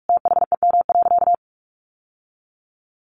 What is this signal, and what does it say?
Morse code, heard as a keyed tone, about 44 wpm, 700 Hz. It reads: THEM.
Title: Sentences from Top 1400 Words 45wpm